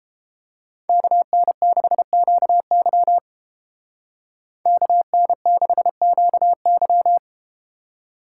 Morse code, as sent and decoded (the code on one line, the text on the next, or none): -.- -. -.... --.- -.-- / -.- -. -.... --.- -.--
KN6QY KN6QY